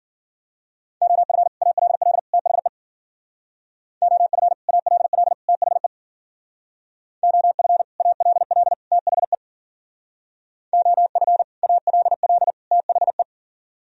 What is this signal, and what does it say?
Morse code, heard as a keyed tone, about 53 wpm, 700 Hz. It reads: OF ALL THE OF ALL THE OF ALL THE OF ALL THE